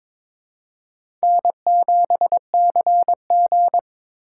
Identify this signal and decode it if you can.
Morse code, heard as a keyed tone, about 22 wpm, 700 Hz. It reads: N7CG
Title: USA Call Signs - 8x Character Spacing - 22wpm